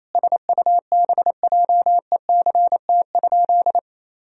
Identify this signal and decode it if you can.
Morse code, heard as a keyed tone, about 28 wpm, 700 Hz. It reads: SUBJECT?
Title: Sentences from Top 1000 Words - Review Part 3 - 28wpm